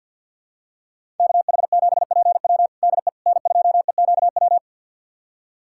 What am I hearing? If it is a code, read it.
Morse code, heard as a keyed tone, about 50 wpm, 700 Hz. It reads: KH7PW DE N2EQW
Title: USA Callsign DE Callsign - 5 Characters - 50wpm